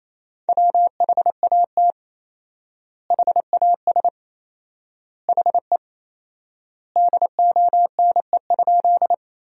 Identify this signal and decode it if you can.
Morse code, heard as a keyed tone, about 28 wpm, 700 Hz. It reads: WHAT HAS HE DONE?
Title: 4x Word-Spacing - Sentences from Top 500 Words - 28wpm